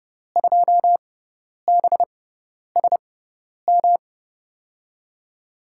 Morse code, heard as a keyed tone, about 30 wpm, spaced as for 10 wpm, 700 Hz. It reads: 2BSM
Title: USA Call Signs - 6 Characters - 6x Character Spacing - 30wpm